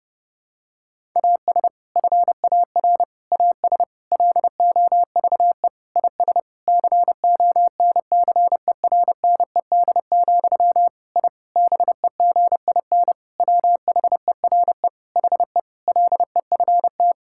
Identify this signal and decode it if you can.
Morse code, heard as a keyed tone, about 30 wpm, 700 Hz. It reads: AS FAR AS LOVE IS CONCERNED, I BEGIN WHERE HE LEFT
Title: Sentences from Top 1000 Words - Review Part 6 - 30wpm